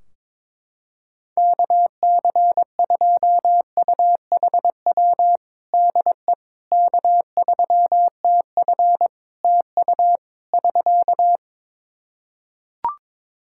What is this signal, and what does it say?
Morse code, heard as a keyed tone, about 21 wpm, 700 Hz. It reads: KC2UHW DE K3TF TU <SK>
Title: Practice QSO - Exchange 4 - Ending - 22wpm